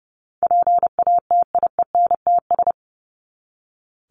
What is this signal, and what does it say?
Morse code, heard as a keyed tone, about 30 wpm, 700 Hz. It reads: PATIENTS
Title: ES:S Suffix - Top 100 Words with Base Words 30wpm